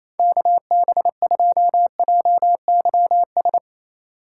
Morse code, heard as a keyed tone, about 28 wpm, 700 Hz. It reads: KB2JYS
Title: USA Call Signs - 6 Characters - 2x Character Spacing - 28wpm